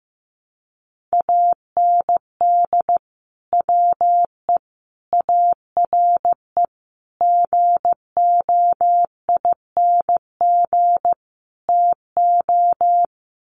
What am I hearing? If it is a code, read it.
Morse code, heard as a keyed tone, about 15 wpm, 700 Hz. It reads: AND WE ARE GOING TO
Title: Sets of 5 Words Encore2 15wpm